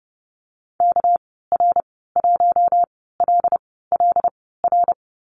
Morse code, heard as a keyed tone, about 30 wpm, spaced as for 17 wpm, 700 Hz. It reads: KR1LLR
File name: Lesson 37g - Letter x - Call Signs 30wpm